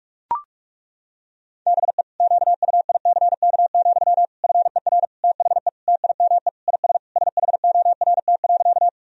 Morse code, heard as a keyed tone, about 41 wpm, 690 Hz. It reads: BE QUICK, FER THE TIME IS SHORT.
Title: Sentences from Top 500 Words and Abbreviations 45wpm